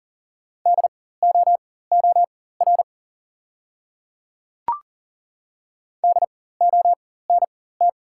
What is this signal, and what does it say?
Morse code, heard as a keyed tone, about 38 wpm, spaced as for 18 wpm, 700 Hz. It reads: DOOR DONT
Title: 4x Character Spacing - Top 300 Words - 40wpm